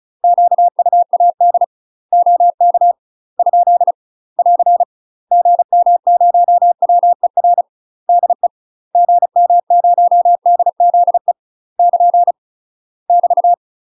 Morse code, heard as a keyed tone, about 36 wpm, 700 Hz. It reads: QUAD OK ? <AR> GM0WER DE GM0DZE <KN> <BT>
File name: UK MASTER QSOs TRAINING / 35WPM / UK MASTER QSOs 0041